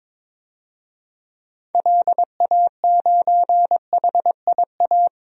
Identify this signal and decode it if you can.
Morse code, heard as a keyed tone, about 22 wpm, 700 Hz. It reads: LA9HIA
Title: DXCC-100 Call Signs - 6 Characters - Repeated 3x - 22wpm